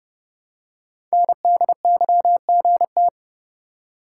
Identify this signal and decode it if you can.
Morse code, heard as a keyed tone, about 30 wpm, 700 Hz. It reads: NDYGT
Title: Code Groups - 5 Letter:Numbers - 30wpm